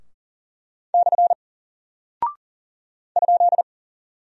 Morse code, heard as a keyed tone, about 39 wpm, 700 Hz. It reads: / ?
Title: Single Character (Letter:Number:Punctuation:Prosign) Alt 40wpm